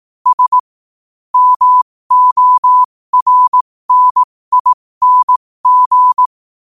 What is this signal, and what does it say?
Morse code, heard as a keyed tone, about 18 wpm, spaced as for 15 wpm, 1000 Hz. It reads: S MORNING
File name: qotdfiles / 15wpm